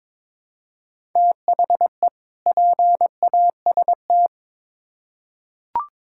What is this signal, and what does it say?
Morse code, heard as a keyed tone, about 22 wpm, 700 Hz. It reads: THE PAST
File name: Build A Phrase - 2-Word Phrases - 22wpm